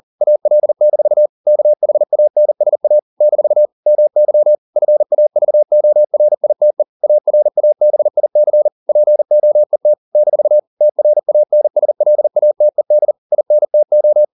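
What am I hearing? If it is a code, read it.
Morse code, heard as a keyed tone, about 40 wpm, 600 Hz. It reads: AL-KHANSA - MY FAVORITE ARABIC POET - TRANSLATED INTO
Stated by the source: Morse code at 40 wpm, practice text, CW med Gnister.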